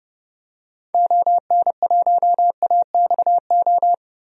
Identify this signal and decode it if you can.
Morse code, heard as a keyed tone, about 30 wpm, 700 Hz. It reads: ON1AXO
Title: DXCC-100 Call Signs - 6 Characters - 30wpm